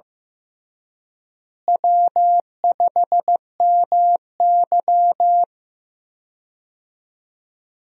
Morse code, heard as a keyed tone, about 15 wpm, 700 Hz. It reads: W5MY
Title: USA Call Signs - 4 Characters - Repeated 3x - 15wpm